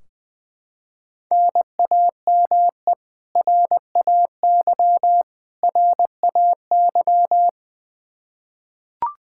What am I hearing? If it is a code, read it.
Morse code, heard as a keyed tone, about 20 wpm, 700 Hz. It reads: NAME RAY RAY
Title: Practice QSO - Exchange 1 - Beginning - 20wpm